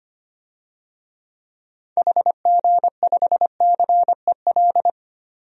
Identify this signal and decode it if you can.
Morse code, heard as a keyed tone, about 25 wpm, 700 Hz. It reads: HG5CEL
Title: DXCC-100 Call Signs - 6 Characters - 25wpm